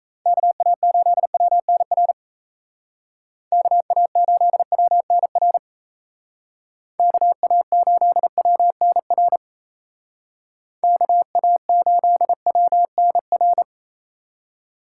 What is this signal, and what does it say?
Morse code, heard as a keyed tone, about 34 wpm, 700 Hz. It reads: KA8WNR KA8WNR KA8WNR KA8WNR
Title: Speed Racing - USA 2x3 Call Signs 28wpm